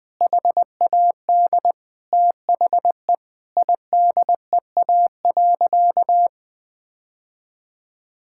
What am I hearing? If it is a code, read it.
Morse code, heard as a keyed tone, about 20 wpm, 700 Hz. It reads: HAD THE IDEA.
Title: Sentences with 2-Word Phrases - 20wpm